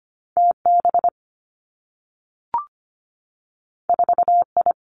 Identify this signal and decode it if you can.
Morse code, heard as a keyed tone, about 25 wpm, 700 Hz. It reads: TB 4I
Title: Two Challenging Letter-Number Combinations 25wpm